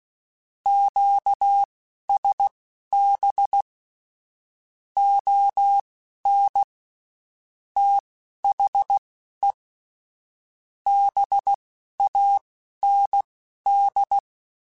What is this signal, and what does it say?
Morse code, heard as a keyed tone, about 16 wpm, spaced as for 12 wpm, 800 Hz. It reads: QSB ON THE BAND
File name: QSOs / QSOs-mp3s / WD0BGP 1@15